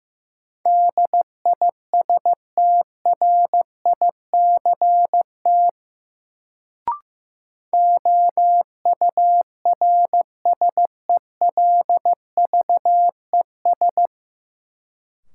Morse code, heard as a keyed tone, about 15 wpm, 700 Hz. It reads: DISTRICT OURSELVES
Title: Sentences from Top 1600 Words - New Word Drill Alt - 15wpm